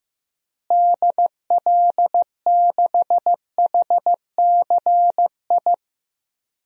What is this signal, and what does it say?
Morse code, heard as a keyed tone, about 15 wpm, 700 Hz. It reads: DL6HCI
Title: DXCC-100 Call Signs - 6 Characters - Repeated 3x - 15wpm